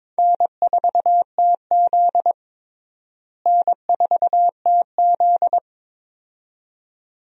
Morse code, heard as a keyed tone, about 22 wpm, 700 Hz. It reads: N4TZ N4TZ
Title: USA Call Signs - 4 Characters - Repeated 2x - 22wpm